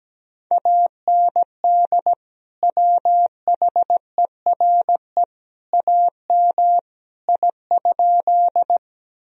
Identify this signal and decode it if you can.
Morse code, heard as a keyed tone, about 17 wpm, 700 Hz. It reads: AND WHERE AM I?
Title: Sentences from Top 1000 Words - Review Part 2 - 17wpm